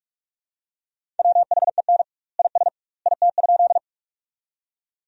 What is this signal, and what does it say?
Morse code, heard as a keyed tone, about 45 wpm, 700 Hz. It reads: WHEN IS IT?
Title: 2x Word-Spacing - Sentences from Top 100 Words - 45wpm